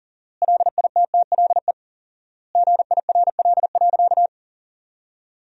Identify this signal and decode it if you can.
Morse code, heard as a keyed tone, about 40 wpm, 700 Hz. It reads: LITTLE GIRL.
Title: 4x Word-Spacing - Sentences from Top 400 Words - 40wpm